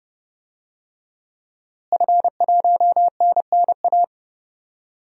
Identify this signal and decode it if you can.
Morse code, heard as a keyed tone, about 30 wpm, 700 Hz. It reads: F1NNA
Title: DXCC-100 Call Signs - 5 Characters - Repeated 3x - 30wpm